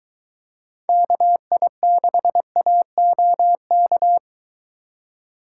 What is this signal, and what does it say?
Morse code, heard as a keyed tone, about 22 wpm, 700 Hz. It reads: KI6AOK